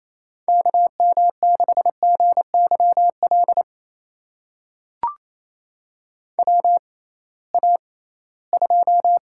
Morse code, heard as a keyed tone, about 27 wpm, spaced as for 18 wpm, 700 Hz. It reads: KM6GYL WA2
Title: USA Call Signs - 6 Characters - 6x Character Spacing - 28wpm